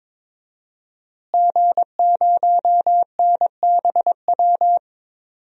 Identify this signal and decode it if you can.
Morse code, heard as a keyed tone, about 22 wpm, 700 Hz. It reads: G0NBW
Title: DXCC-100 Call Signs - 5 Characters - Repeated 3x - 22wpm